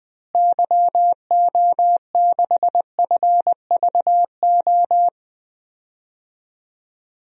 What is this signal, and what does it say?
Morse code, heard as a keyed tone, about 20 wpm, 700 Hz. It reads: YO6FVO
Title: DXCC-100 Call Signs - 6 Characters - Repeated 3x - 20wpm